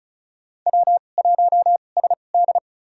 Morse code, heard as a keyed tone, about 34 wpm, spaced as for 25 wpm, 700 Hz. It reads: W1SD
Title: USA Call Signs - 2x Character Spacing - 35wpm